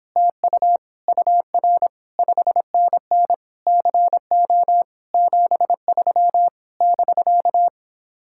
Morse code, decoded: TU UR 5NN CO 73 <BK>

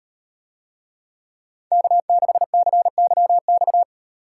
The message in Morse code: K6CYX

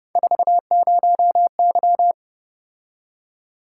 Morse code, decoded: 40Y

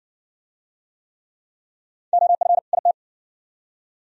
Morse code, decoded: OF IT

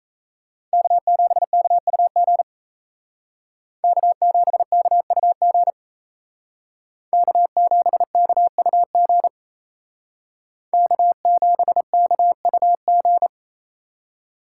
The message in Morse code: K7KUG K7KUG K7KUG K7KUG

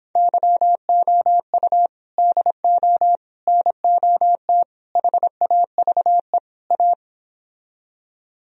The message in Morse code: YOU DO NOT HAVE A

